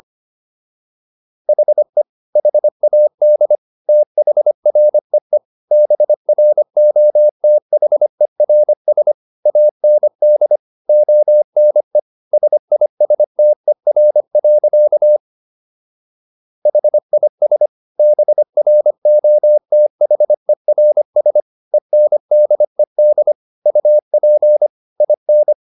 Morse code, heard as a keyed tone, about 25 wpm, 600 Hz. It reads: HE HAD THREE BROTHERS AND ONE SISTER. HIS BROTHERS ENDED UP IN